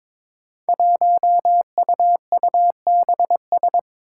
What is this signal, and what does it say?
Morse code, heard as a keyed tone, about 22 wpm, 700 Hz. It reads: 1UUBS
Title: Five Challenging Letter-Number Combinations 22wpm